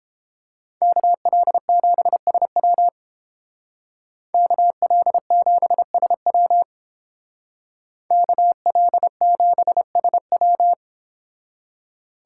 Morse code, heard as a keyed tone, about 29 wpm, 700 Hz. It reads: KL7SW KL7SW KL7SW